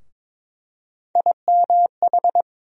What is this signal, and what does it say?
Morse code, heard as a keyed tone, about 22 wpm, 700 Hz. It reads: IMH
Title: USA Call Sign Postfixes - 3 Letters 22wpm